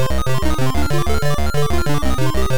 There is no code.